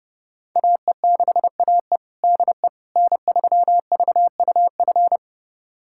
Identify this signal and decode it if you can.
Morse code, heard as a keyed tone, about 30 wpm, 700 Hz. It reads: AE6AE DE N3VUF
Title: USA Callsign DE Callsign - 5 Characters - 30wpm